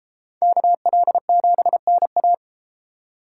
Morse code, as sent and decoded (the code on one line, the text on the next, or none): -.- .-.. --... -. .-
KL7NA